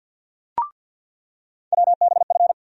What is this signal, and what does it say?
Morse code, heard as a keyed tone, about 53 wpm, 700 Hz. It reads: WBF
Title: USA Call Sign Postfixes - 3 Letters 50wpm